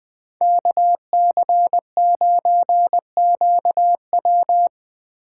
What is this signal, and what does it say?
Morse code, heard as a keyed tone, about 22 wpm, 700 Hz. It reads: KC9QW
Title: Speed Racing - USA 2x2 Call Signs 20wpm